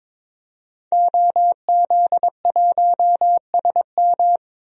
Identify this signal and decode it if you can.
Morse code, heard as a keyed tone, about 22 wpm, 700 Hz. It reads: OZ1SM
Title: DXCC-100 Call Signs - 5 Characters - 22wpm